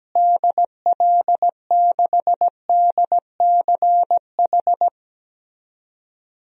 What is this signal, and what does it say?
Morse code, heard as a keyed tone, about 17 wpm, 700 Hz. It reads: DL6DCH
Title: DXCC-100 Call Signs - 6 Characters - Repeated 3x - 17wpm